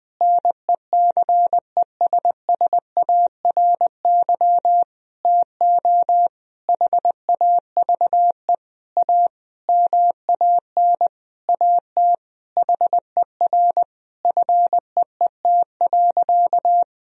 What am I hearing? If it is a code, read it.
Morse code, heard as a keyed tone, about 21 wpm, 700 Hz. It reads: NECESSARY TO HAVE A MAN AT HER FEET.
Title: Sentences from Top 1000 Words - Review Part 1 - 20wpm